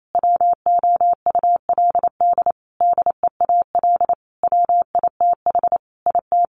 Morse code, heard as a keyed tone, about 28 wpm, 700 Hz. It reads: WOULD DEAL WITH IT
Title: Sentences from Top 1500 Words 28wpm